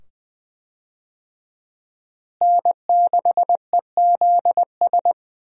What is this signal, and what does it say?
Morse code, heard as a keyed tone, about 20 wpm, 700 Hz. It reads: N6EZS
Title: USA Call Signs - 5 Characters - Repeated 2x - 20wpm